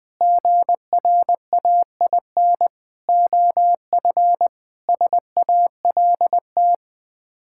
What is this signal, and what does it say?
Morse code, heard as a keyed tone, about 20 wpm, 700 Hz. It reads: GRAIN OF SALT
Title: Sets of 8 Words - 20wpm